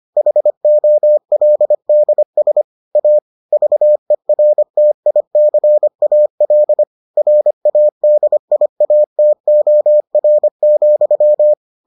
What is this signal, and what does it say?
Morse code, heard as a keyed tone, about 25 wpm, 600 Hz. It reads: HOLDS A VERTICAL RADIATOR,